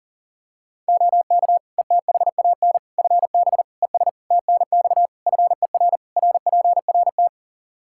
Morse code, heard as a keyed tone, about 40 wpm, 700 Hz. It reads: OK ETHAN FB ES TNX FER RPRT
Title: Practice QSO - Exchange 2 - Info - 40wpm